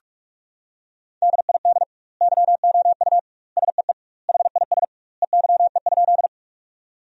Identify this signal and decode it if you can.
Morse code, heard as a keyed tone, about 46 wpm, 700 Hz. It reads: DID YOU SEE HIS EYE?
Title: 2x Word-Spacing - Sentences from Top 300 Words - 45wpm